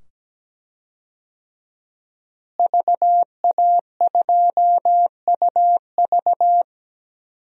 Morse code, heard as a keyed tone, about 17 wpm, 700 Hz. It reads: VA2UV
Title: DXCC-100 Call Signs - 5 Characters - Repeated 2x - 17wpm